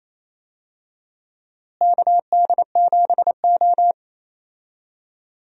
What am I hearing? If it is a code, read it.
Morse code, heard as a keyed tone, about 28 wpm, 700 Hz. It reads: KD7O
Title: USA Call Signs - 4 Characters - Repeated 3x - 28wpm